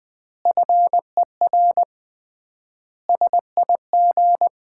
Morse code, heard as a keyed tone, about 20 wpm, 700 Hz. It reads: FER SIG
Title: Common QSO Phrases - 3x Word-Spacing - 20wpm